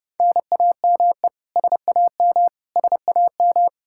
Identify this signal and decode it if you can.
Morse code, heard as a keyed tone, about 30 wpm, 700 Hz. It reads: NAME SAM SAM
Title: Practice QSO - Exchange 1 - Beginning - 30wpm